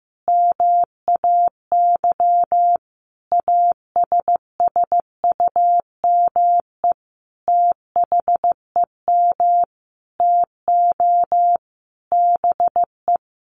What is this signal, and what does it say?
Morse code, heard as a keyed tone, about 15 wpm, 700 Hz. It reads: MAY ASSUME THEM TO BE
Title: Sentences from Top 1500 Words 15wpm